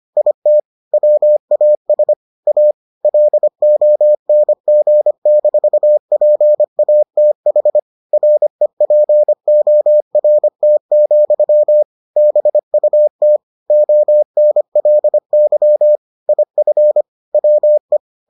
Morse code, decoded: IT WAS A LONG-PATH REPORT, BUT ONLY IF WE

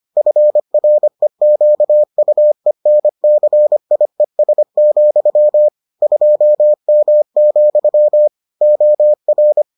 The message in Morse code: FREQUENCIES, 2M, OR

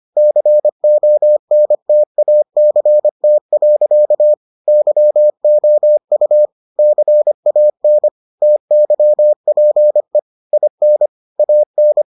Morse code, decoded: CONTACT. YOU CAN TYPE IN AN

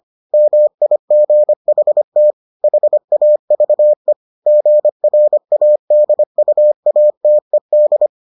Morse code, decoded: MIGHT HAVE GRADUATED

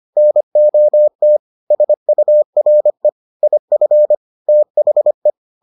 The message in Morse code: NOT SURE IF THE

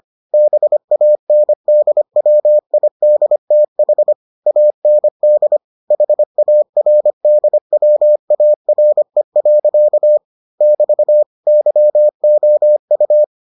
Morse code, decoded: BANDWIDTH AND HARDWARE. <BT> YOU

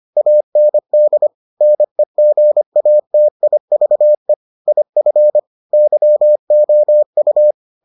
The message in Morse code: AND NEGATIVE IF YOU

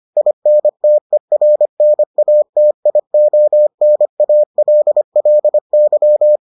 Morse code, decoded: INTERNATIONALLY

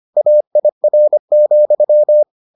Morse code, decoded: AIR,

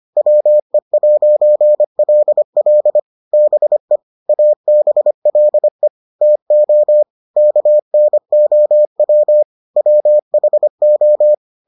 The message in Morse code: WE'LL BE ABLE TO KNOW WHO

